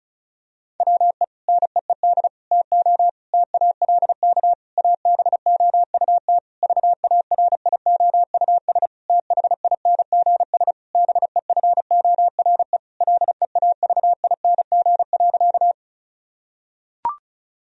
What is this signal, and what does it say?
Morse code, heard as a keyed tone, about 34 wpm, 700 Hz. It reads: WE NEED TO TALK ABOUT VARIOUS THINGS BEFORE LEAVING.
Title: Sentences from Top 1000 Words - Review Part 4 - 35wpm